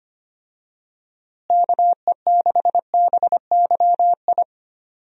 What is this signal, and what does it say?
Morse code, heard as a keyed tone, about 25 wpm, 700 Hz. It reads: KE6BYI
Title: USA Call Signs - 6 Characters - Repeated 3x - 25wpm